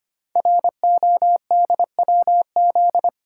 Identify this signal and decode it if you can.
Morse code, heard as a keyed tone, about 25 wpm, 700 Hz. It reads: RODWZ
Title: Code Groups - 5 Letters - 25wpm